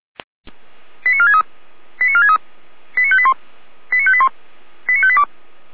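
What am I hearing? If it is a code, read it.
Five-tone selective call (ZVEI), heard as quick runs of five tones, tones of about 70 ms. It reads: 89463 89463 89581 89581 89582
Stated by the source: ZVEI five-tone call received by SDR, archived by feuerwehr-orlamuende.org fire brigade server.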